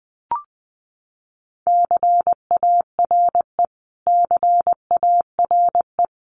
Morse code, heard as a keyed tone, about 20 wpm, 700 Hz. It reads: CARE CARE
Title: Top 300 Words Repeated 20wpm